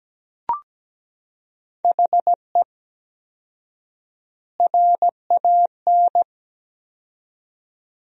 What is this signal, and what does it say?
Morse code, heard as a keyed tone, about 17 wpm, 700 Hz. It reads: HE RAN
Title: 4x Word-Spacing - Sentences from Top 500 Words - 17wpm